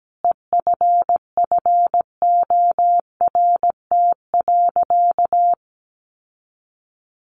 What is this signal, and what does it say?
Morse code, heard as a keyed tone, about 17 wpm, 700 Hz. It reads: EFFORT.
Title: Sentences from Top 1500 Words 17wpm